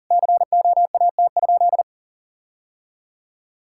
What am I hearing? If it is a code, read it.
Morse code, heard as a keyed tone, about 40 wpm, 700 Hz. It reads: COAT?
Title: Sentences from Top 1600 Words 40wpm